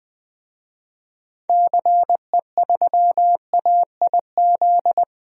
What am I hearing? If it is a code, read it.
Morse code, heard as a keyed tone, about 20 wpm, 700 Hz. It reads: CE3AIZ
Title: DXCC-100 Call Signs - 6 Characters - 20wpm